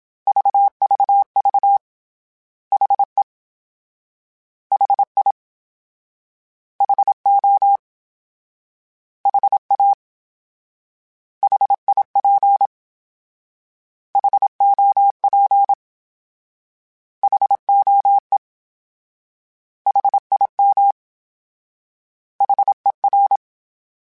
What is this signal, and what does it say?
Morse code, heard as a keyed tone, about 27 wpm, 800 Hz. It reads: VVV HE HI HO HA HIP HOP HOE HIM HER
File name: H Prefixes / H mp3s